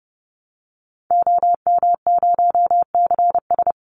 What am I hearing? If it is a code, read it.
Morse code, heard as a keyed tone, about 30 wpm, 700 Hz. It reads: OM0CS
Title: Call Signs Repeated 2x - 30wpm